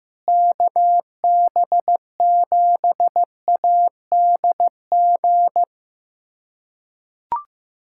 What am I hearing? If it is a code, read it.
Morse code, heard as a keyed tone, about 15 wpm, 700 Hz. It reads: KB7ADG
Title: USA Callsign DE Callsign - 6 Characters - 15wpm